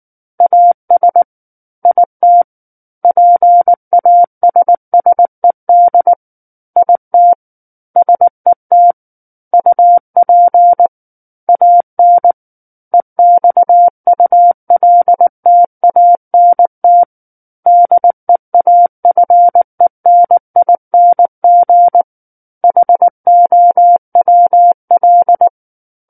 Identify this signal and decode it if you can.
Morse code, heard as a keyed tone, about 19 wpm, 700 Hz. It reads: AS IT PASSED IT SET UP AN EXULTANT DEAFENING HOWL
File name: Book One / 10-Chapter 10 - 19 WPM